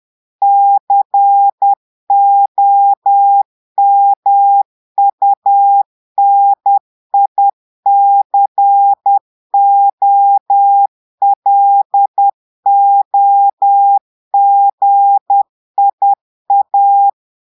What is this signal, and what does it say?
Morse code, heard as a keyed tone, about 10 wpm, 800 Hz. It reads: COMUNICOLOGIA
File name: Portugais - texte en clair 1 - vitesse - 10 mots minute